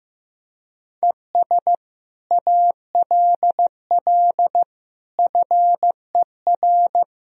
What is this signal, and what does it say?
Morse code, heard as a keyed tone, about 15 wpm, 700 Hz. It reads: ES ALL FER
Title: Sentences from Top 100 Words and Abbreviations 15wpm